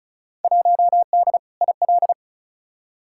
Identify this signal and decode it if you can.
Morse code, heard as a keyed tone, about 36 wpm, 700 Hz. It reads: 1D IL
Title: Field Day Exchanges - 35wpm